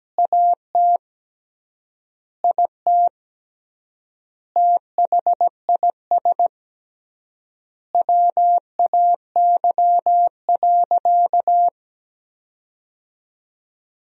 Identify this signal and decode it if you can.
Morse code, heard as a keyed tone, about 17 wpm, 700 Hz. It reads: AT IT THIS WAY.
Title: 3x Word-Spacing - Sentences from Top 100 Words - 17wpm